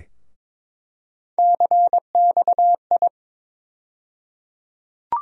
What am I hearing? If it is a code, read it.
Morse code, heard as a keyed tone, about 22 wpm, 700 Hz. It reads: CXI
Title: USA Call Sign Postfixes - 3 Letters 22wpm